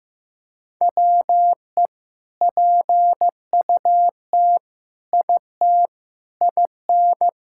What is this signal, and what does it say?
Morse code, heard as a keyed tone, about 15 wpm, 700 Hz. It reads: WE PUT IT IN
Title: Sentences from Top 1000 Words - Review Part 5 - 15wpm